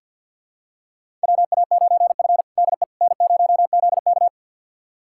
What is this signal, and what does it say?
Morse code, heard as a keyed tone, about 52 wpm, 700 Hz. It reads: WA9F DE N0ZK
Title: USA Callsign DE Callsign - 4 Characters - 50wpm